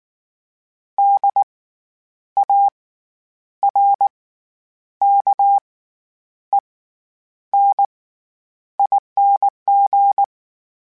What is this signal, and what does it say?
Morse code, heard as a keyed tone, about 19 wpm, spaced as for 9 wpm, 800 Hz. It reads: DARKENING
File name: Special INGs / Special INGs-mp3s / SlowToFaster ing @18